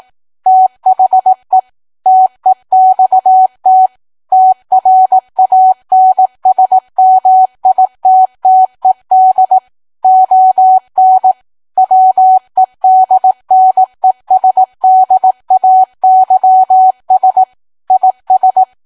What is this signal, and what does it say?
Morse code, heard as a keyed tone, about 18 wpm, 750 Hz. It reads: THE TEXT TRANSMITTED ON WEDNESDAYS IS